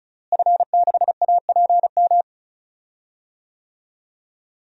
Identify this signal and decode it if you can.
Morse code, heard as a keyed tone, about 36 wpm, 700 Hz. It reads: F6APM
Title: DXCC-100 Call Signs - 5 Characters - Repeated 3x - 35wpm